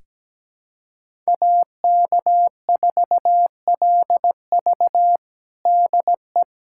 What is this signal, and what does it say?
Morse code, heard as a keyed tone, about 17 wpm, 700 Hz. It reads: AK4LV DE
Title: USA Callsign DE Callsign - 5 Characters - 17wpm